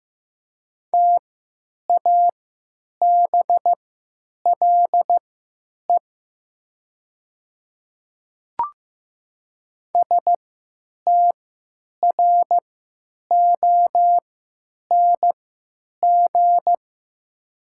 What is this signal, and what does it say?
Morse code, heard as a keyed tone, about 15 wpm, spaced as for 9 wpm, 700 Hz. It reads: TABLE STRONG
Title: 3x Character Spacing - Top 500 Words - 15wpm